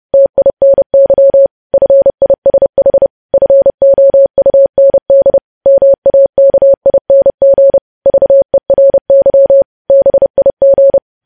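Morse code, heard as a keyed tone, about 30 wpm, 550 Hz. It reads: TINY FISH FOUND MAKING VERY BIG
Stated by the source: News headlines sent in Morse at 30 WPM.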